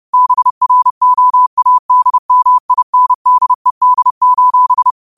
Morse code, heard as a keyed tone, about 30 wpm, 1000 Hz. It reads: BROADMINDED:
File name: qotdfiles / 30wpm